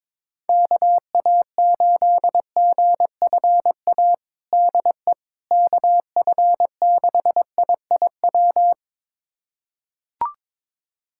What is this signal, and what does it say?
Morse code, heard as a keyed tone, about 21 wpm, 700 Hz. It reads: KA8GFA DE KF6IIW
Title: USA Callsign DE Callsign 22wpm